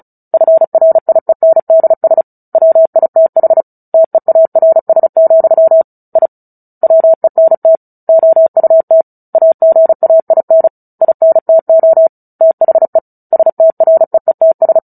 Morse code, heard as a keyed tone, about 34 wpm, 650 Hz. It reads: FRIENDS WITH TEARS, I WENT OUT AGAIN INTO THE STREETS
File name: Book Two / 09-Chapter 9 - 35 WPM